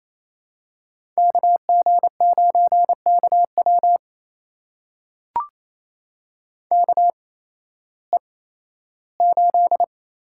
Morse code, heard as a keyed tone, about 28 wpm, spaced as for 15 wpm, 700 Hz. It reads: KG9KW KE8
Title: USA Call Signs - 8x Character Spacing - 28wpm